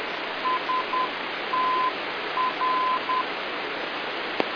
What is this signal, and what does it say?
Morse code, heard as a keyed tone, about 10 wpm, 1020 Hz, with noise. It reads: STR